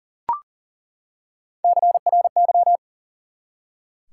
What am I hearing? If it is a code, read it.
Morse code, heard as a keyed tone, about 42 wpm, 700 Hz. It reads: CRY
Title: Three Letter Words - 40wpm